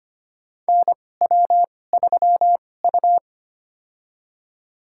Morse code, heard as a keyed tone, about 25 wpm, spaced as for 18 wpm, 700 Hz. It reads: NW3U
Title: USA Call Signs - 4 Characters - 2x Character Spacing - 25wpm